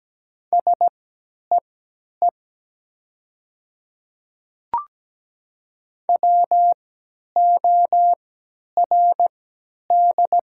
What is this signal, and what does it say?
Morse code, heard as a keyed tone, about 17 wpm, spaced as for 10 wpm, 700 Hz. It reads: SEE WORD
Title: 3x Character Spacing - Top 200 Words - 17wpm